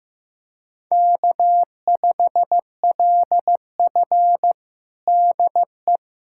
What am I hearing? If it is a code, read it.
Morse code, heard as a keyed tone, about 15 wpm, 700 Hz. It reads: K5LF DE
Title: USA Callsign DE Callsign - 4 Characters - 15wpm